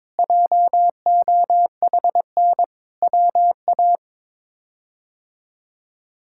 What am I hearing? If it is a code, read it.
Morse code, heard as a keyed tone, about 22 wpm, 700 Hz. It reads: JOHN WA
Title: K1USN SST QSO Practice - 22wpm